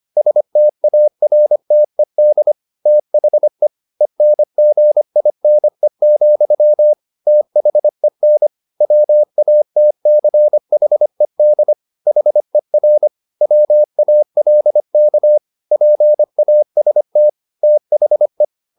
Morse code, heard as a keyed tone, about 25 wpm, 600 Hz. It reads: STARTED THE ENGINE, THEN WATCHED HER WALK PAST THE